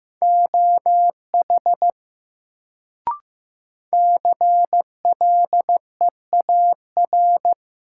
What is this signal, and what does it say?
Morse code, heard as a keyed tone, about 15 wpm, 700 Hz. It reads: OH CLEAR
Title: Sentences from Top 500 Words - New Word Drill - 15wpm